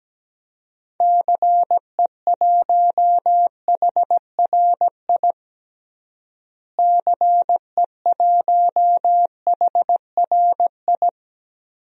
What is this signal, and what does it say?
Morse code, heard as a keyed tone, about 17 wpm, 700 Hz. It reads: CE1HRI CE1HRI
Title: DXCC-100 Call Signs - 6 Characters - Repeated 2x - 17wpm